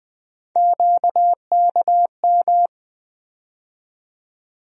Morse code, heard as a keyed tone, about 20 wpm, 700 Hz. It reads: QKM
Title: USA Call Sign Postfixes - 3 Letters 20wpm